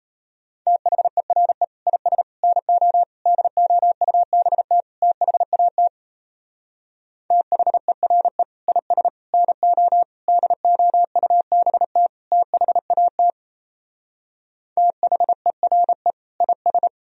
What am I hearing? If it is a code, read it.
Morse code, heard as a keyed tone, about 35 wpm, 700 Hz. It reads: THERE IS NO DOUBT THAT THERE IS NO DOUBT THAT THERE IS
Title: Speed-Racing - Sets of 5 Words 28wpm